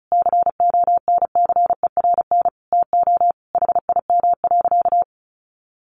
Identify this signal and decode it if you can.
Morse code, heard as a keyed tone, about 34 wpm, 700 Hz. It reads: CONCERN TO HIM.
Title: Sentences from Top 1200 Words 35wpm